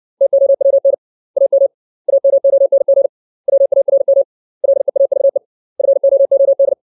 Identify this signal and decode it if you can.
Morse code, heard as a keyed tone, about 60 wpm, 550 Hz. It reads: TOWN AM AMONG WARM LEAVE FOOD
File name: 632 words / 632 Words 60WPM